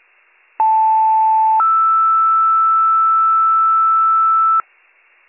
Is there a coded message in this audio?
Yes: two-tone page from 0.6 to 4.6 s, heard as a one-second tone, then a three-second tone, about 880 then 1320 Hz, with noise.